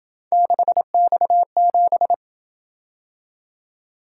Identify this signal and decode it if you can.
Morse code, heard as a keyed tone, about 26 wpm, 700 Hz. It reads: 6X7